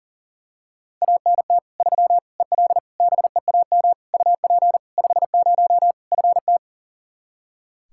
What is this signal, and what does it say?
Morse code, heard as a keyed tone, about 37 wpm, 700 Hz. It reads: ANT 3 EL BEAM UP 50 FT
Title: Practice QSO - Exchange 2 - Info - 40wpm